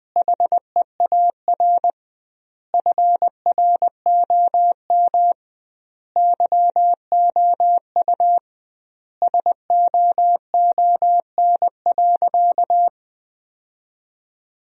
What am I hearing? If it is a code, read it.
Morse code, heard as a keyed tone, about 20 wpm, 700 Hz. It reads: HEAR FROM YOU SOON.
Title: 2x Word-Spacing - Sentences from Top 400 Words - 20wpm